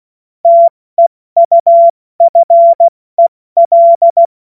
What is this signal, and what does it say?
Morse code, heard as a keyed tone, about 16 wpm, 680 Hz. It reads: TEUFEL
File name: Morsekurs / Witze / Witz 25. Oktober 2022 70 F80